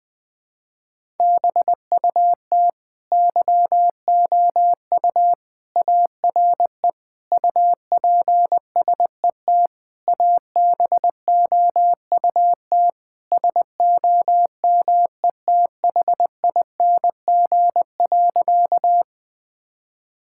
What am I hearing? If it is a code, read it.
Morse code, heard as a keyed tone, about 20 wpm, 700 Hz. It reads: BUT YOU ARE UPSET ABOUT SOMETHING.
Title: Sentences from Top 1000 Words - Review Part 5 - 20wpm